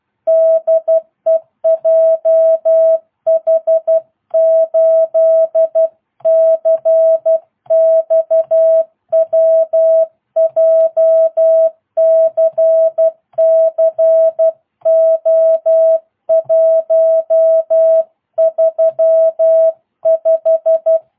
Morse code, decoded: DE JH8CXW JCC O135

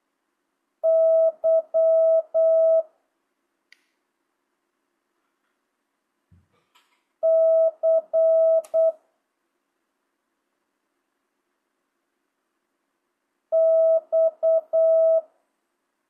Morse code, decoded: Y C X

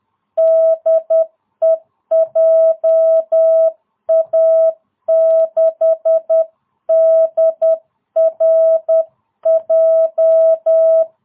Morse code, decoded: DE JA6DRJ

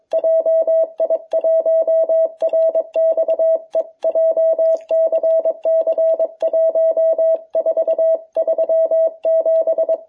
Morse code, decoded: JI1RXE JCC 1437

